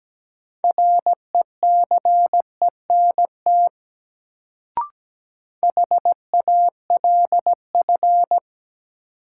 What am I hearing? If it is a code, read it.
Morse code, heard as a keyed tone, about 17 wpm, 700 Hz. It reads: RECENT HALF
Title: Sets of 3 Words - New Word Drill - 17wpm